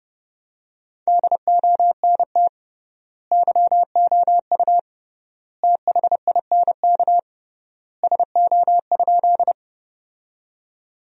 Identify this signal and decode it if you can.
Morse code, heard as a keyed tone, about 32 wpm, 700 Hz. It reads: DONT YOU THINK SO?
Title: 3x Word-Spacing - Sentences from Top 300 Words - 30wpm